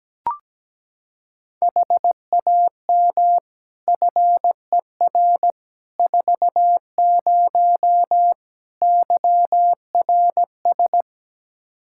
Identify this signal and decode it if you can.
Morse code, heard as a keyed tone, about 17 wpm, 700 Hz. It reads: HAM FER 40 YRS
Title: Practice QSO - Exchange 3 - More Info - 17wpm